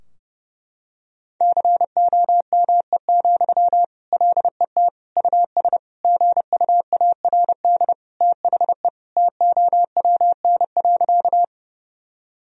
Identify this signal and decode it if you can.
Morse code, heard as a keyed tone, about 30 wpm, 700 Hz. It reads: COME, LET US GUARD THE TOWN.
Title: Sentences from Top 1000 Words - Review Part 4 - 30wpm